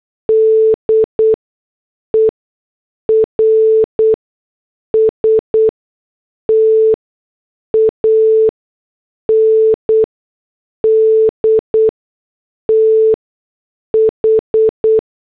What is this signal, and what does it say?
Morse code, heard as a keyed tone, about 8 wpm, spaced as for 6 wpm, 440 Hz. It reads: DERSTAND TH